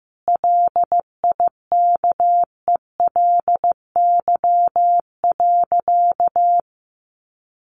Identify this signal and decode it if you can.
Morse code, heard as a keyed tone, about 15 wpm, 700 Hz. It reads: LIKELY.
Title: Sentences from Top 600 Words 15wpm